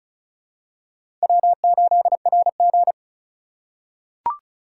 Morse code, heard as a keyed tone, about 34 wpm, 700 Hz. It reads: W8RG
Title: USA Call Signs - 4 Characters - 4x Character Spacing - 35wpm